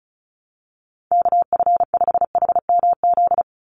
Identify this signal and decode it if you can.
Morse code, heard as a keyed tone, about 34 wpm, 700 Hz. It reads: KF5HMZ
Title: USA Call Signs 35wpm